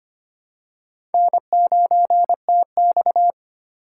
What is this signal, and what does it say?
Morse code, heard as a keyed tone, about 25 wpm, 700 Hz. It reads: N9TX
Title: USA Call Signs - 2x Character Spacing - 25wpm